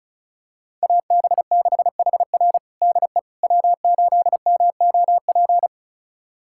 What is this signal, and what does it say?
Morse code, heard as a keyed tone, about 36 wpm, 700 Hz. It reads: AB6HR DE W8MOP
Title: USA Callsign DE Callsign - 5 Characters - 35wpm